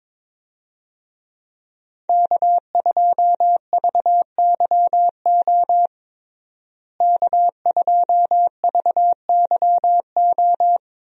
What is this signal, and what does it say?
Morse code, heard as a keyed tone, about 22 wpm, 700 Hz. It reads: K2VYO K2VYO
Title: USA Call Signs - 5 Characters - Repeated 3x - 22wpm